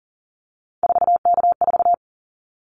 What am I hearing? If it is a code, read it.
Morse code, heard as a keyed tone, about 40 wpm, 700 Hz. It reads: 4K4
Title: Callsign Prefixes for Top 100 DXCC - 40wpm